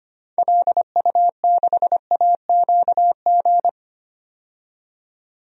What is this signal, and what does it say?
Morse code, heard as a keyed tone, about 25 wpm, 700 Hz. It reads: LU6AQG
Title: DXCC-100 Call Signs - 6 Characters - 25wpm